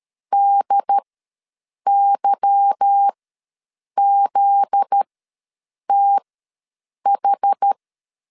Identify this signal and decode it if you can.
Morse code, heard as a keyed tone, about 13 wpm, spaced as for 7 wpm, 800 Hz. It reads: DYZTH
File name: LEZIONI / 8 YJ45 / 8 YJ4 ws 12wpm 3249